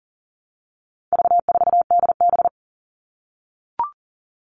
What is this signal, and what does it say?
Morse code, heard as a keyed tone, about 40 wpm, 700 Hz. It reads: V4DB
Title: All Cut Numbers - 4-Digit 40wpm